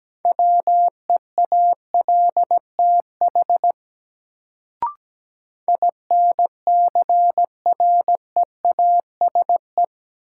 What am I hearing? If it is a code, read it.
Morse code, heard as a keyed tone, about 17 wpm, 700 Hz. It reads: WEALTH INCREASE
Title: Sentences from Top 1000 Words - Review Part 4 - New Word Drill - 17wpm